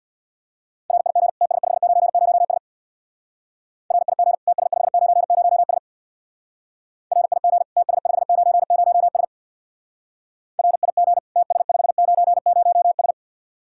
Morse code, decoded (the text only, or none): RIG TS590S RIG TS590S RIG TS590S RIG TS590S